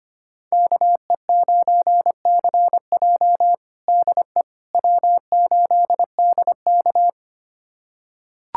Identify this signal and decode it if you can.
Morse code, heard as a keyed tone, about 25 wpm, 700 Hz. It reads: KE9CJ DE W8DK
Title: USA Callsign DE Callsign 25wpm